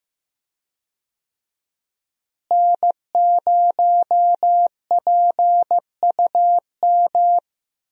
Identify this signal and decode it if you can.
Morse code, heard as a keyed tone, about 15 wpm, 700 Hz. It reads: N0PUM
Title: USA Call Signs - 5 Characters - Repeated 2x - 15wpm